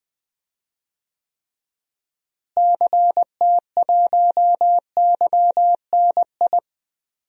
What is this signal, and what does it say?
Morse code, heard as a keyed tone, about 21 wpm, 700 Hz. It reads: CT1YNI
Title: DXCC-100 Call Signs - 6 Characters - Repeated 2x - 20wpm